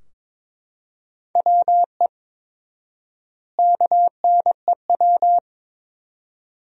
Morse code, heard as a keyed tone, about 21 wpm, 700 Hz. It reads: WE KNEW
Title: 4x Word-Spacing - Sentences from Top 400 Words - 22wpm